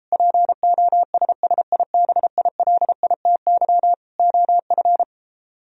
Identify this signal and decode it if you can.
Morse code, heard as a keyed tone, about 33 wpm, 700 Hz. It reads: POSSIBILITY OF